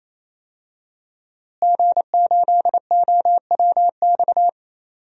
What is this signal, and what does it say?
Morse code, heard as a keyed tone, about 28 wpm, 700 Hz. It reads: G8OWX
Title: DXCC-100 Call Signs - 5 Characters - Repeated 3x - 28wpm